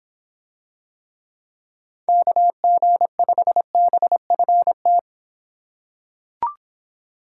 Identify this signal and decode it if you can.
Morse code, heard as a keyed tone, about 24 wpm, 700 Hz. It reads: KG5BFT